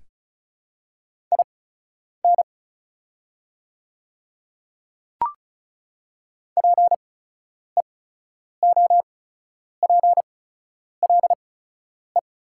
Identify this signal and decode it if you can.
Morse code, heard as a keyed tone, about 36 wpm, spaced as for 10 wpm, 690 Hz. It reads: IN PEOPLE
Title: 8x Character Spacing - Top 100 Words - 35wpm